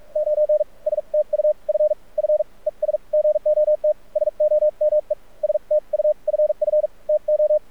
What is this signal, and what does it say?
Morse code, heard as a keyed tone, about 46 wpm, spaced as for 36 wpm, 610 Hz. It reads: 9 STUFF ES GOT SOME STUFF TO
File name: archived / CRSnet